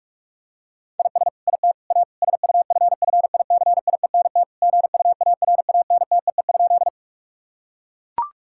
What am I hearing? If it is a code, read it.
Morse code, heard as a keyed tone, about 41 wpm, 700 Hz. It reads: IS IT A SUFFICIENT GUARANTEE?
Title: Sentences from Top 1600 Words 45wpm